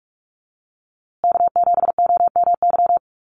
Morse code, heard as a keyed tone, about 44 wpm, 690 Hz. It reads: K7OMY
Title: USA Call Signs Repeated 3x 45wpm